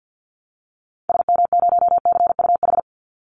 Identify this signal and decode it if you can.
Morse code, heard as a keyed tone, about 52 wpm, 700 Hz. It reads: SM0CUH
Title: Call Signs Repeated 3x - 50wpm